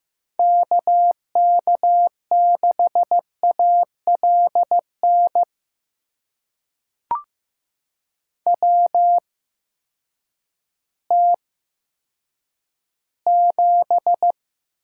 Morse code, heard as a keyed tone, about 15 wpm, spaced as for 8 wpm, 700 Hz. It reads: KK6ALN WT7
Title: USA Call Signs - 8x Character Spacing - 15wpm